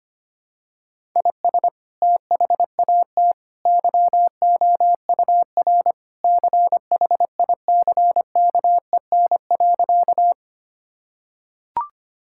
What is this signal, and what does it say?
Morse code, heard as a keyed tone, about 25 wpm, 700 Hz. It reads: IS THAT YOUR CHICKEN.
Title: Sentences from Top 1000 Words - Review Part 5 - 25wpm